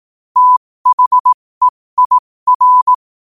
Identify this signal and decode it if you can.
Morse code, heard as a keyed tone, about 18 wpm, spaced as for 15 wpm, 1000 Hz. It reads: THEIR
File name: qotdfiles / 15wpm